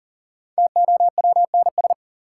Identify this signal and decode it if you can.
Morse code, heard as a keyed tone, about 40 wpm, 700 Hz. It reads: TOWNS
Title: Sentences from Top 1000 Words - Review Part 6 - New Word Drill - 40wpm